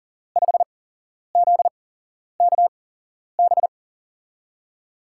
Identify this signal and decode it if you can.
Morse code, heard as a keyed tone, about 40 wpm, spaced as for 11 wpm, 700 Hz. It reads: 5ZKB
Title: USA Call Signs - 5 Characters - 8x Character Spacing - 40wpm